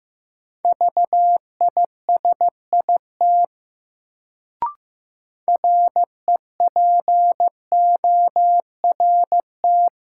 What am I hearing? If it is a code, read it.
Morse code, heard as a keyed tone, about 15 wpm, 700 Hz. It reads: VISIT REPORT
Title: Sentences from Top 900 Words - New Word Drill - 15wpm